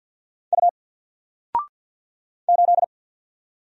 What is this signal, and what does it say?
Morse code, heard as a keyed tone, about 50 wpm, 700 Hz. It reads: U 8
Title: Single Character (Letter:Number:Punctuation:Prosign) 50wpm